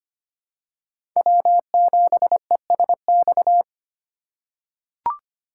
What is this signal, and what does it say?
Morse code, heard as a keyed tone, about 25 wpm, 700 Hz. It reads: W7ESX
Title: USA Call Signs - 5 Characters - 3x Character Spacing - 25wpm